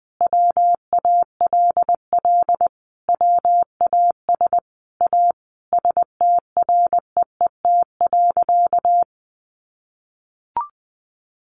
Morse code, heard as a keyed tone, about 20 wpm, 700 Hz. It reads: WALL WAS A STREET.
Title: Sentences from Top 1100 Words 20wpm